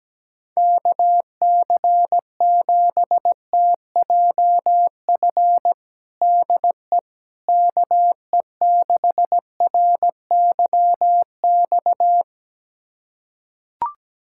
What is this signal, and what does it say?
Morse code, heard as a keyed tone, about 17 wpm, 700 Hz. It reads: KC7TJF DE KE6RYX
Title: USA Callsign DE Callsign - 6 Characters - 17wpm